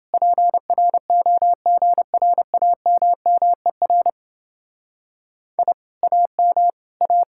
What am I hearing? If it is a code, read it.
Morse code, heard as a keyed tone, about 28 wpm, 700 Hz. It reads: PROGRAMMER I AM A